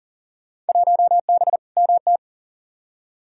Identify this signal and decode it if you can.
Morse code, heard as a keyed tone, about 40 wpm, 700 Hz. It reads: 1B MT
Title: Field Day Exchanges - 40wpm